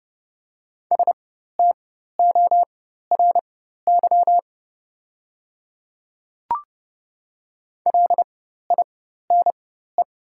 Morse code, heard as a keyed tone, about 29 wpm, spaced as for 14 wpm, 690 Hz. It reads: STORY LINE
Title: 4x Character Spacing - Top 500 Words - 30wpm